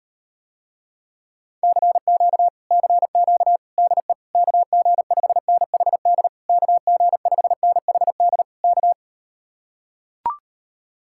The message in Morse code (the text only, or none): CQ CQ DE KG5NHD KG5NHD K